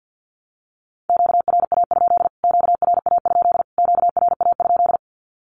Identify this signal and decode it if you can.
Morse code, heard as a keyed tone, about 48 wpm, 700 Hz. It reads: QRA? QRA? QRA?
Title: Q-Codes Repeated 3x 50wpm